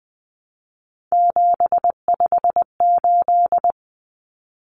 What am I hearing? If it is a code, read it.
Morse code, heard as a keyed tone, about 19 wpm, 700 Hz. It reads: 758
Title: Numbers - 3-Digit 20wpm